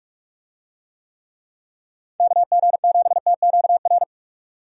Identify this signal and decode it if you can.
Morse code, heard as a keyed tone, about 43 wpm, 700 Hz. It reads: KG7TQR
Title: USA Call Signs - 6 Characters - 45wpm